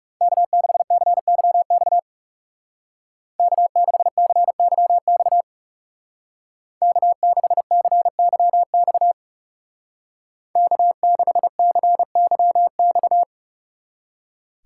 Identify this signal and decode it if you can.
Morse code, heard as a keyed tone, about 40 wpm, 700 Hz. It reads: K6CYX K6CYX K6CYX K6CYX